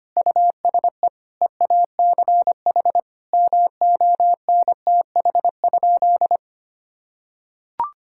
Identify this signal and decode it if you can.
Morse code, heard as a keyed tone, about 25 wpm, 700 Hz. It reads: USE EACH MONTH?
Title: Sentences from Top 1000 Words - Review Part 6 - 25wpm